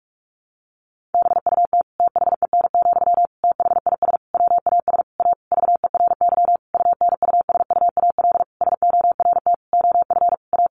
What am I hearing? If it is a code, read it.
Morse code, heard as a keyed tone, about 44 wpm, 700 Hz. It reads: BUT THEN, THIS WAS A VERY UNUSUAL SORT OF A
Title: Sentences from Top 1400 Words 45wpm